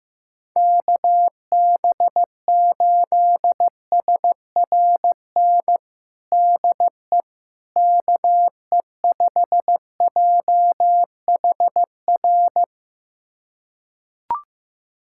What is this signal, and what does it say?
Morse code, heard as a keyed tone, about 15 wpm, 700 Hz. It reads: KB8SRN DE KE5JHR
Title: USA Callsign DE Callsign - 6 Characters - 15wpm